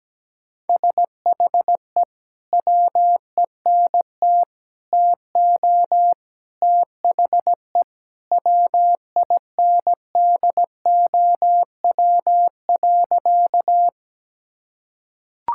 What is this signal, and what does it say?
Morse code, heard as a keyed tone, about 17 wpm, 700 Hz. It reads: SHE WENT TO THE WINDOW.
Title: Sentences from Top 1600 Words 17wpm